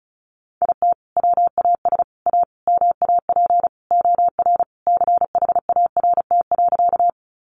Morse code, heard as a keyed tone, about 34 wpm, 700 Hz. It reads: IT WAS A MAP OR CHART.
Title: Sentences from Top 1300 Words 35wpm